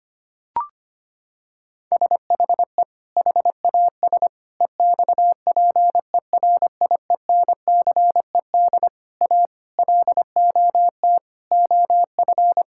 Morse code, heard as a keyed tone, about 25 wpm, 700 Hz. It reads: SHE HAS EXPERIENCED A LOT OF
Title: Sentences from Top 1600 Words 25wpm